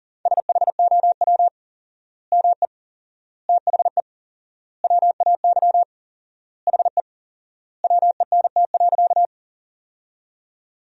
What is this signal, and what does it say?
Morse code, heard as a keyed tone, about 40 wpm, 700 Hz. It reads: SHOW ME THE WAY HE WENT.
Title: 4x Word-Spacing - Sentences from Top 200 Words - 40wpm